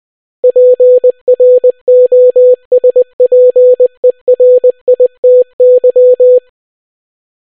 Morse code, heard as a keyed tone, about 20 wpm, 500 Hz. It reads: PROSPERITY